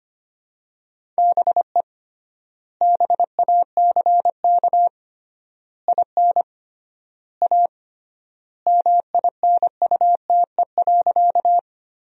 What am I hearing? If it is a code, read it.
Morse code, heard as a keyed tone, about 25 wpm, 700 Hz. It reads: BE BACK IN A MINUTE.
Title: 3x Word-Spacing - Sentences from Top 500 Words - 25wpm